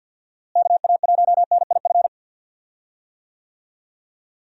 Morse code, heard as a keyed tone, about 50 wpm, 700 Hz. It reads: KA1NIF
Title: USA Call Signs - 6 Characters - Repeated 3x - 50wpm